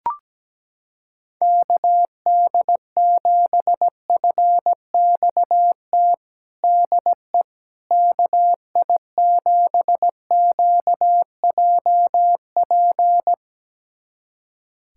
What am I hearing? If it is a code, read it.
Morse code, heard as a keyed tone, about 17 wpm, 700 Hz. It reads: KD7FXT DE KI7QJP
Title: USA Callsign DE Callsign - 6 Characters - 17wpm